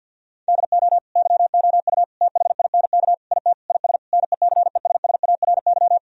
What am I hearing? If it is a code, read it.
Morse code, heard as a keyed tone, about 50 wpm, 700 Hz. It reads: DO YOU THINK IT IS NECESSARY
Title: Sentences from Top 1000 Words - Review Part 1 - 50wpm